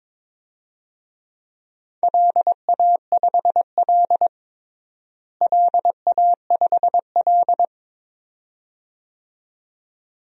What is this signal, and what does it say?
Morse code, heard as a keyed tone, about 22 wpm, 700 Hz. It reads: LA5L LA5L
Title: DXCC-100 Call Signs - 4 Characters - Repeated 2x - 22wpm